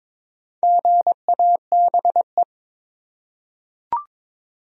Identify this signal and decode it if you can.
Morse code, heard as a keyed tone, about 22 wpm, 700 Hz. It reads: GABE
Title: USA Common Names - Max 5 Letters - 22wpm